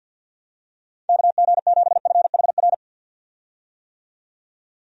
Morse code, heard as a keyed tone, about 52 wpm, 700 Hz. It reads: KG7FHR
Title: USA Call Signs - 6 Characters - 50wpm